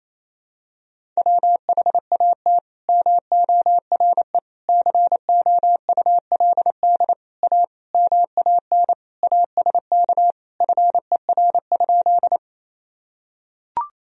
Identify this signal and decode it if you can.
Morse code, heard as a keyed tone, about 27 wpm, 700 Hz. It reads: WHAT MORE COULD A MAN ASK FER?
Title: Sentences from Top 200 Words and Abbreviations 28wpm